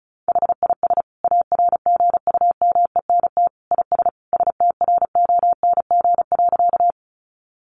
Morse code, heard as a keyed tone, about 34 wpm, 700 Hz. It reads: HIS ARGUMENT IS STRONG.
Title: Sentences from Top 1400 Words 35wpm